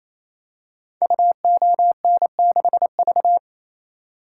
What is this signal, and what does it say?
Morse code, heard as a keyed tone, about 28 wpm, 700 Hz. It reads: UON6V
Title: Code Groups - 5 Letter:Numbers - 28wpm